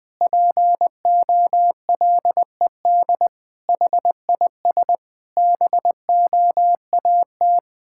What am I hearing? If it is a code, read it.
Morse code, heard as a keyed tone, about 20 wpm, 700 Hz. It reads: POLED HIS BOAT
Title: Pangram Challenge v2 - 20wpm